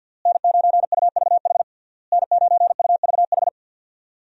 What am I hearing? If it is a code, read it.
Morse code, heard as a keyed tone, about 52 wpm, 700 Hz. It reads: N9UVH N9UVH
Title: USA Call Signs - 5 Characters - Repeated 3x - 50wpm